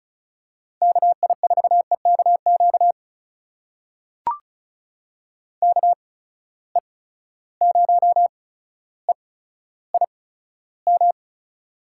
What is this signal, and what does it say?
Morse code, heard as a keyed tone, about 35 wpm, spaced as for 15 wpm, 700 Hz. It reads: KI4EKQ KE0EIM
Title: USA Call Signs - 6 Characters - 8x Character Spacing - 35wpm